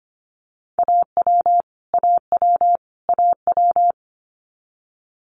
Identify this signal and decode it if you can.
Morse code, heard as a keyed tone, about 25 wpm, 700 Hz. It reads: AW AW AW
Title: Multi-letter Phonemes Repeated 3x 25wpm